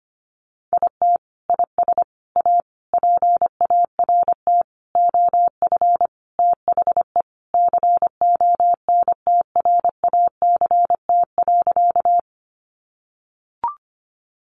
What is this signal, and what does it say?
Morse code, heard as a keyed tone, about 25 wpm, 700 Hz. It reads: IT IS A PART OF THE CONTRACT.
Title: Sentences from Top 1400 Words 25wpm